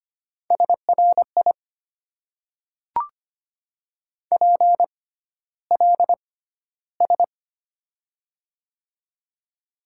Morse code, heard as a keyed tone, about 25 wpm, spaced as for 13 wpm, 690 Hz. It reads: SRI PLS
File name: Top 90 QSO Elements - 6x Character Spacing - 25wpm